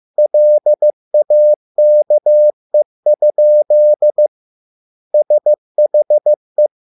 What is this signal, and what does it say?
Morse code, heard as a keyed tone, about 15 wpm, 600 Hz. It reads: LAKE? SHE